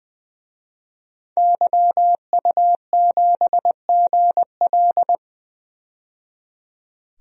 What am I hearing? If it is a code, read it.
Morse code, heard as a keyed tone, about 20 wpm, 700 Hz. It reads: YU7GL
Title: DXCC-100 Call Signs - 5 Characters - 20wpm